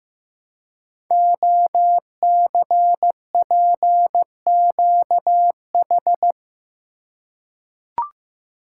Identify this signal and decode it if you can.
Morse code, heard as a keyed tone, about 15 wpm, 700 Hz. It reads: OCPQH
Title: Code Groups - 5 Letter:Numbers - 15wpm